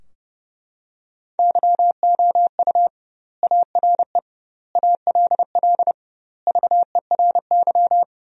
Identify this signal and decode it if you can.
Morse code, heard as a keyed tone, about 31 wpm, 700 Hz. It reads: YOU ARE ALL VERY
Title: 2x Word-Spacing - Sentences from Top 200 Words - 30wpm